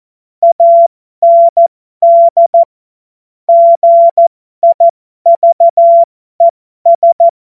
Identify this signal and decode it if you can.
Morse code, heard as a keyed tone, about 14 wpm, spaced as for 12 wpm, 680 Hz. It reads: AND GIVES
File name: Morsekurs / Witze